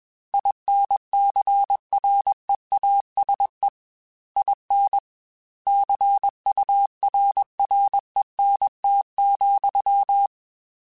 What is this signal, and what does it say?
Morse code, heard as a keyed tone, about 21 wpm, 800 Hz. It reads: INCREASE IN CURRENT,